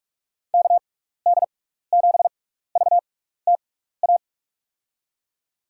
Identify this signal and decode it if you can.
Morse code, heard as a keyed tone, about 44 wpm, spaced as for 15 wpm, 700 Hz. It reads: KD7VTA
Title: USA Call Signs - 6 Characters - 6x Character Spacing - 45wpm